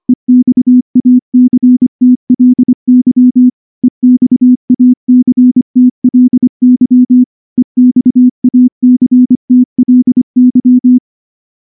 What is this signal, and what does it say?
Morse code, heard as a keyed tone, about 24 wpm, 250 Hz. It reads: EXACTLY EXACTLY EXACTLY
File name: morse / morse-training-25WPM